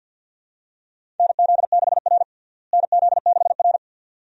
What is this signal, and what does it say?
Morse code, heard as a keyed tone, about 46 wpm, 700 Hz. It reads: NZ6R NZ6R
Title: USA Call Signs - 4 Characters - Repeated 2x - 50wpm